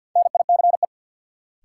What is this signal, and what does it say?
Morse code, heard as a keyed tone, about 52 wpm, 700 Hz. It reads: NICE
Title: Sentences from Top 1100 Words - New Word Drill Alt - 50wpm